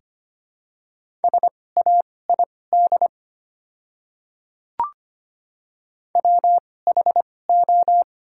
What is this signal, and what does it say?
Morse code, heard as a keyed tone, about 25 wpm, spaced as for 18 wpm, 700 Hz. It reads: SAID WHO
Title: 2x Character Spacing - Top 100 Words - 25wpm